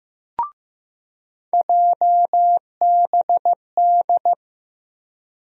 Morse code, heard as a keyed tone, about 15 wpm, 700 Hz. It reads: JBD
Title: USA Call Sign Postfixes - 3 Letters 15wpm